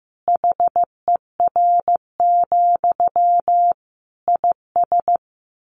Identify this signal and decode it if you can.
Morse code, heard as a keyed tone, about 14 wpm, 700 Hz. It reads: HER, IS
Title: Sentences from Top 1500 Words 15wpm